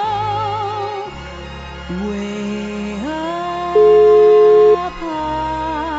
Ringback tone (Europe, 425 Hz), with noise.